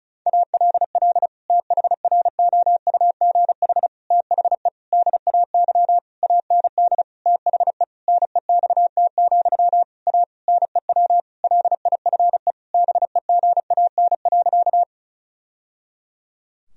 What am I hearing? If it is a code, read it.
Morse code, heard as a keyed tone, about 34 wpm, 700 Hz. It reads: ALL THROUGH THE DAY AND THE NEXT, A NEW LIFE BEGAN.
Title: Sentences from Top 1000 Words - Review Part 1 - 35wpm